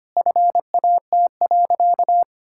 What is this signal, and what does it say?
Morse code, heard as a keyed tone, about 25 wpm, 700 Hz. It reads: FAT.
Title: Sentences from Top 1000 Words - Review Part 5 - 25wpm